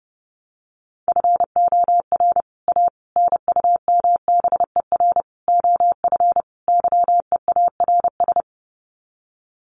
Morse code, heard as a keyed tone, about 30 wpm, 700 Hz. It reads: FOR A NUMBER OF YEARS
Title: Sets of 5 Words Encore 30wpm